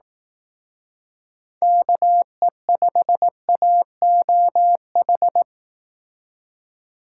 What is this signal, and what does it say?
Morse code, heard as a keyed tone, about 18 wpm, 700 Hz. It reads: KE5AOH
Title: Speed Racing - USA 2x3 Call Signs 15wpm